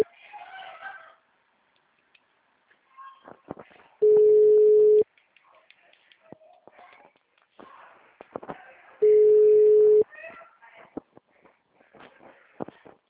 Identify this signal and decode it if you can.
Ringback tone (Europe, 425 Hz), with speech.